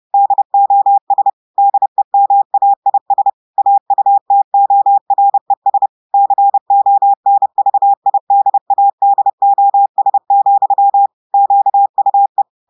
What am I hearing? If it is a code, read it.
Morse code, heard as a keyed tone, about 30 wpm, 800 Hz. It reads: DOS DEMAIS AUTORES CONVIDADOS, QUE
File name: audios / portugais / portugais texte en clair 1 vitesse 30 mots minutes